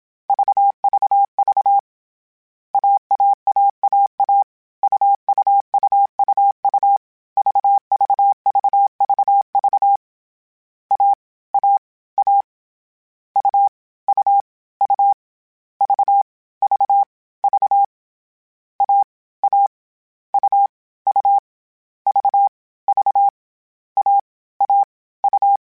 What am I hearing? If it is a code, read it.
Morse code, heard as a keyed tone, about 27 wpm, 800 Hz. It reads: VVV AAAAA UUUUU VVVVV A A A U U U V V V A A U U V V A A U
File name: Difficult Combos / Difficult Combos-mp3s / AUV AUV4 @25